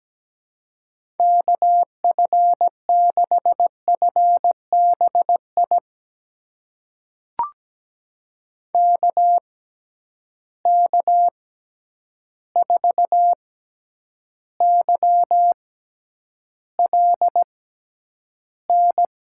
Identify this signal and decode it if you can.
Morse code, heard as a keyed tone, about 17 wpm, spaced as for 9 wpm, 700 Hz. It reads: KF6FBI KK4YLN
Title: USA Call Signs - 6 Characters - 6x Character Spacing - 17wpm